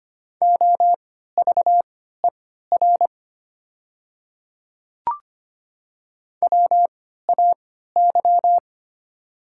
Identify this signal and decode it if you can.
Morse code, heard as a keyed tone, about 25 wpm, spaced as for 14 wpm, 700 Hz. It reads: OVER WAY
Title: 3x Character Spacing - Top 100 Words - 25wpm